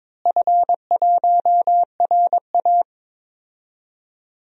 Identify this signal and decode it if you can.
Morse code, heard as a keyed tone, about 22 wpm, 700 Hz. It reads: F1RA
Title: DXCC-100 Call Signs - 4 Characters - Repeated 2x - 22wpm